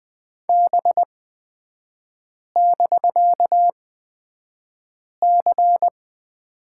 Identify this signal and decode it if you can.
Morse code, heard as a keyed tone, about 20 wpm, spaced as for 5 wpm, 700 Hz. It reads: B<BK>C
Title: 250 Lesson 40e - BK - Three Characters 20wpm